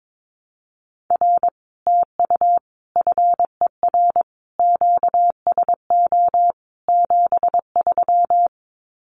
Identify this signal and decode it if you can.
Morse code, heard as a keyed tone, about 22 wpm, 700 Hz. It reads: R TU FER QSO 73
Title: Common QSO Phrases Encore 22wpm